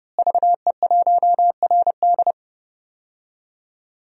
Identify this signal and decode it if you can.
Morse code, heard as a keyed tone, about 30 wpm, 700 Hz. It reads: VE1RD
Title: DXCC-100 Call Signs - 5 Characters - Repeated 3x - 30wpm